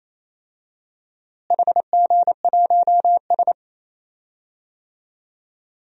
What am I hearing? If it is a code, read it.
Morse code, heard as a keyed tone, about 28 wpm, 700 Hz. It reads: HG1S
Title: DXCC-100 Call Signs - 4 Characters - Repeated 3x - 28wpm